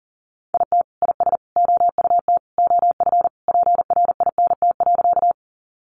Morse code, heard as a keyed tone, about 40 wpm, 700 Hz. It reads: IT IS OUT OF PRINT.
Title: Sentences from Top 1400 Words 40wpm